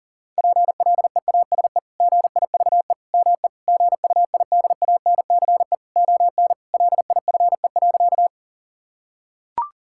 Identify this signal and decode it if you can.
Morse code, heard as a keyed tone, about 40 wpm, 700 Hz. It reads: PLEASE GIVE ME GUIDANCE ON LIFE.
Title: Sentences from Top 1600 Words 40wpm